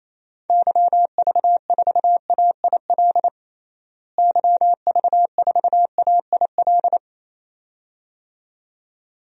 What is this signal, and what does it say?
Morse code, heard as a keyed tone, about 28 wpm, 700 Hz. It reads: YV4AIL YV4AIL
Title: DXCC-100 Call Signs - 6 Characters - Repeated 2x - 28wpm